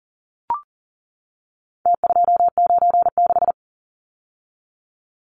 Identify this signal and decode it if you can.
Morse code, heard as a keyed tone, about 40 wpm, 700 Hz. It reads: T296
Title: Numbers - Contest Style - 4-Digit 40wpm